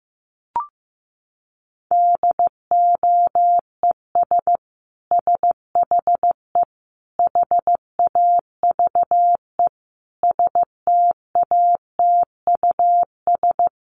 Morse code, heard as a keyed tone, about 15 wpm, 700 Hz. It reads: DOES SHE HAVE STATUS
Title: Sentences from Top 1400 Words 15wpm